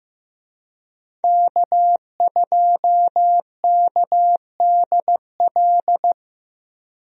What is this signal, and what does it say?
Morse code, heard as a keyed tone, about 15 wpm, 700 Hz. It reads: K2KDL
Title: USA Call Signs - 5 Characters - Repeated 2x - 15wpm